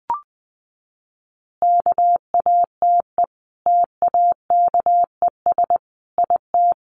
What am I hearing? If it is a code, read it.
Morse code, heard as a keyed tone, about 20 wpm, 700 Hz. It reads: KATE TAKES IT
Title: Easy Phrases 20wpm